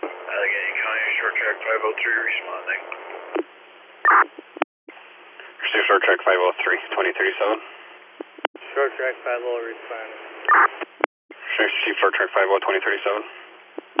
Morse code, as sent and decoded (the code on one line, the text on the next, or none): none